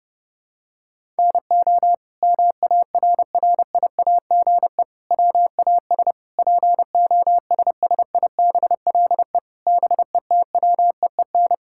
Morse code, heard as a keyed tone, about 30 wpm, 700 Hz. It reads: NO MARRIAGE WAS POSSIBLE BETWEEN
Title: Sentences from Top 1000 Words - Review Part 5 - 30wpm